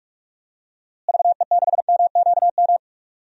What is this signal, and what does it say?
Morse code, heard as a keyed tone, about 42 wpm, 700 Hz. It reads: VE6MQM
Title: DXCC-100 Call Signs - 6 Characters - Repeated 2x - 45wpm